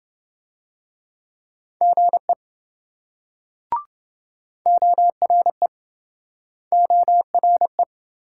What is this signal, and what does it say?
Morse code, heard as a keyed tone, about 27 wpm, 700 Hz. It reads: GE ORE ORE